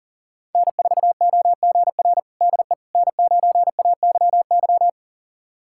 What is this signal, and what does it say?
Morse code, heard as a keyed tone, about 40 wpm, 700 Hz. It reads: N4OGR DE N9AYY
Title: USA Callsign DE Callsign - 5 Characters - 40wpm